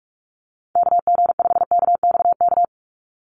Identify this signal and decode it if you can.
Morse code, heard as a keyed tone, about 44 wpm, 700 Hz. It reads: KG5KXK
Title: USA Call Signs 45wpm